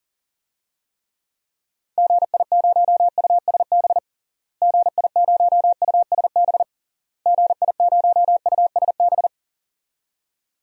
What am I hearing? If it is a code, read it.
Morse code, heard as a keyed tone, about 37 wpm, 700 Hz. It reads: GI0USB GI0USB GI0USB
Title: DXCC-100 Call Signs - 6 Characters - Repeated 3x - 40wpm